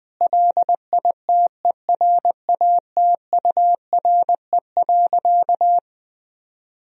Morse code, heard as a keyed tone, about 21 wpm, 700 Hz. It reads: LITERATURE.
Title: Sentences from Top 1000 Words - Review Part 3 - 20wpm